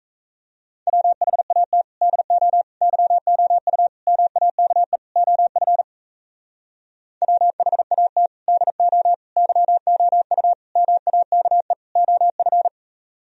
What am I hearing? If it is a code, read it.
Morse code, heard as a keyed tone, about 38 wpm, 700 Hz. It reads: WHAT DO YOU MAKE OF WHAT DO YOU MAKE OF